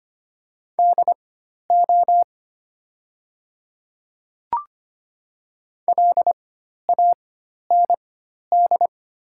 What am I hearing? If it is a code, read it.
Morse code, heard as a keyed tone, about 25 wpm, spaced as for 12 wpm, 700 Hz. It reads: DO LAND
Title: 4x Character Spacing - Top 200 Words - 25wpm